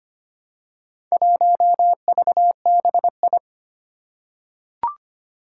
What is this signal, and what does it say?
Morse code, heard as a keyed tone, about 25 wpm, 700 Hz. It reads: 1VBI
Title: Four Challenging Letter-Number Combinations 25wpm